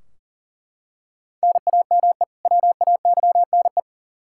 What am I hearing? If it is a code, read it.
Morse code, heard as a keyed tone, about 40 wpm, 700 Hz. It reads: NAME WAYNE
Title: Practice QSO - Exchange 1 - Beginning - 40wpm